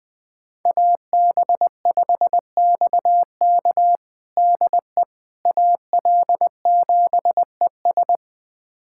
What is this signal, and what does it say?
Morse code, heard as a keyed tone, about 20 wpm, 700 Hz. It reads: AB5XK DE AL7ES
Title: USA Callsign DE Callsign - 5 Characters - 20wpm